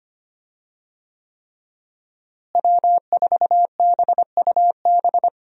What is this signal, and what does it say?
Morse code, heard as a keyed tone, about 25 wpm, 700 Hz. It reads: W4BUB
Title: USA Call Signs - 5 Characters - Repeated 3x - 25wpm